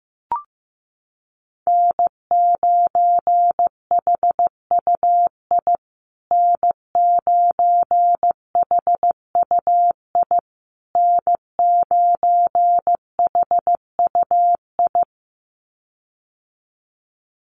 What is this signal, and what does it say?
Morse code, heard as a keyed tone, about 15 wpm, 700 Hz. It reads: N9HUI N9HUI N9HUI
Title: USA Call Signs Repeated 3x 15wpm